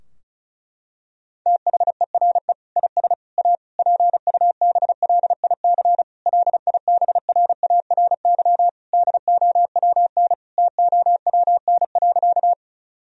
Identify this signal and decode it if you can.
Morse code, heard as a keyed tone, about 36 wpm, 700 Hz. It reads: THERE IS A PUBLIC LIBRARY DOWN TOWN.
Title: Sentences from Top 1000 Words - Review Part 4 - 35wpm